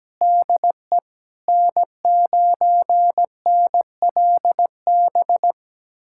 Morse code, heard as a keyed tone, about 17 wpm, 700 Hz. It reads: DE N9NLB
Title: USA Callsign DE Callsign 17wpm